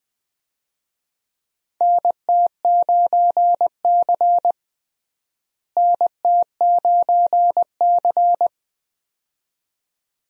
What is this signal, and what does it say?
Morse code, heard as a keyed tone, about 21 wpm, 700 Hz. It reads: NT9C NT9C
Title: USA Call Signs - 4 Characters - Repeated 2x - 20wpm